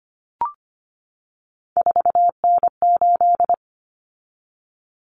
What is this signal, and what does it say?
Morse code, heard as a keyed tone, about 25 wpm, 700 Hz. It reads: 4N8
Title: Common Cut Numbers - 3-Digit 25wpm